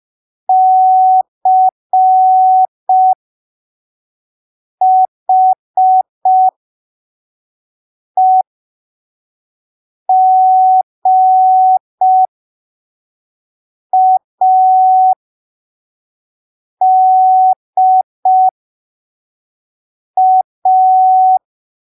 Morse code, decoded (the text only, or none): CHEGADA